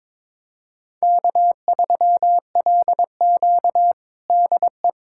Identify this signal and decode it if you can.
Morse code, heard as a keyed tone, about 22 wpm, 700 Hz. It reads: K3LQ DE
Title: USA Callsign DE Callsign - 4 Characters - 22wpm